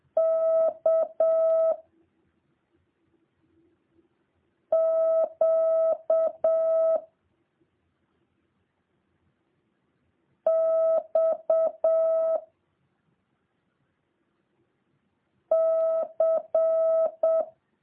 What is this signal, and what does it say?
Morse code, heard as a keyed tone, about 7 wpm, 650 Hz. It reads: K Q X C